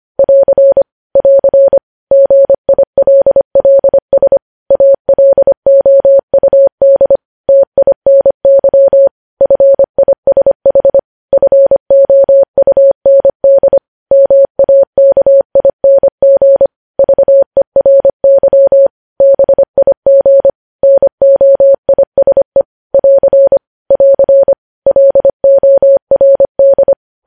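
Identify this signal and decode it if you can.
Morse code, heard as a keyed tone, about 25 wpm, 550 Hz. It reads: <AR> <AR> GILLS ALOUD TINY FISH FOUND MAKING VERY BIG NOISE <AR> <AR> LORD
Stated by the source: Morse code news headlines at 25 wpm.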